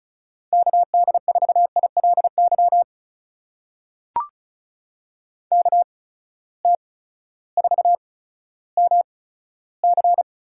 Morse code, heard as a keyed tone, about 36 wpm, 700 Hz. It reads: K D 4 I L Y K T 4 M C
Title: USA Call Signs - 8x Character Spacing - 35wpm